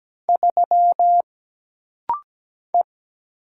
Morse code, heard as a keyed tone, about 17 wpm, 700 Hz. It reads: 3 E
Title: Single Letter-Number Alt - Rapid-Fire 17wpm